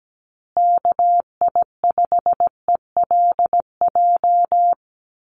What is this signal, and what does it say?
Morse code, heard as a keyed tone, about 17 wpm, 700 Hz. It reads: KI5ELJ
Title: USA Call Signs Repeated 2x 17wpm